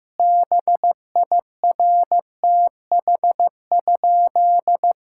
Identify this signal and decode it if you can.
Morse code, heard as a keyed tone, about 15 wpm, 700 Hz. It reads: BIRTH?
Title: Sentences from Top 1000 Words - Review Part 5 - 15wpm